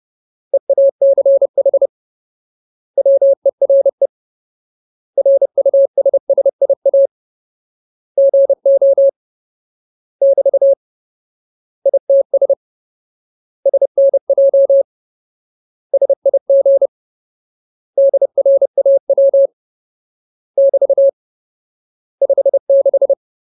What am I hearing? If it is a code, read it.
Morse code, heard as a keyed tone, about 30 wpm, 550 Hz. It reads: EACH WERE RUSSIA GO <BT> ITS SNJ SIG DRAW <BT> 56
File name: Ultimate Word List / 30WPM Ultimate